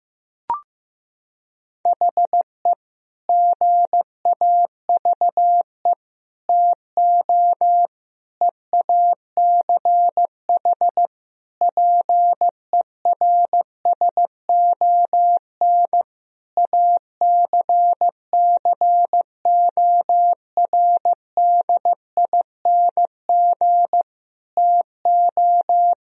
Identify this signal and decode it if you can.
Morse code, heard as a keyed tone, about 15 wpm, 700 Hz. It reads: HE GAVE TO EACH PERSON ACCORDING TO
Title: Sentences from Top 1000 Words - Review Part 4 - 15wpm